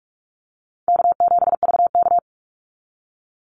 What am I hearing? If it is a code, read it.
Morse code, heard as a keyed tone, about 44 wpm, 700 Hz. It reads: K7VK
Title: NA SOTA Activator Callsigns 45wpm